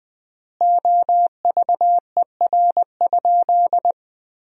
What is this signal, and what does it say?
Morse code, heard as a keyed tone, about 21 wpm, 700 Hz. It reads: OVER?
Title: 4x Word-Spacing - Sentences from Top 100 Words - 20wpm